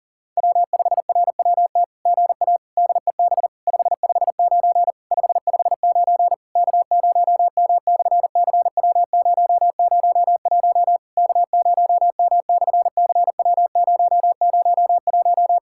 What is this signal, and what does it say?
Morse code, heard as a keyed tone, about 40 wpm, 700 Hz. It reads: W5RWT GA DEB 559 559 K0M/CW001 K0M/CW001
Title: SOTA QSO Practice 40wpm